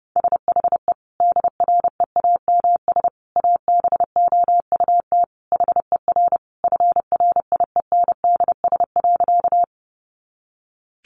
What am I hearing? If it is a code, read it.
Morse code, heard as a keyed tone, about 30 wpm, 700 Hz. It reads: SHE DREAMS ABOUT HER FRIENDS.
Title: Easy Phrases 30wpm